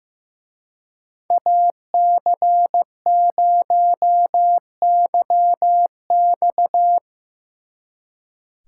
Morse code, heard as a keyed tone, about 17 wpm, 700 Hz. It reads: AC0YX
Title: Speed Racing - USA 2x2 Call Signs 15wpm